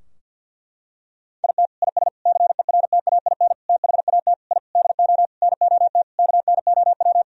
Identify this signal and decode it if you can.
Morse code, heard as a keyed tone, about 51 wpm, 700 Hz. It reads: IT IS CERTAIN THAT I DO NOT KNOW
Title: Sentences from Top 1000 Words - Review Part 2 - 50wpm